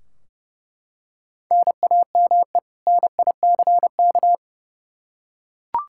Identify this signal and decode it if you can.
Morse code, heard as a keyed tone, about 32 wpm, 700 Hz. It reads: NAME NICK
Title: Practice QSO - Exchange 1 - Beginning - 30wpm